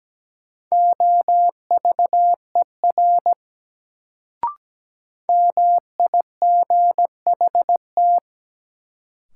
Morse code, heard as a keyed tone, about 17 wpm, 700 Hz. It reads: OVER MIGHT
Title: Sets of 2 Words - New Word Drill - 17wpm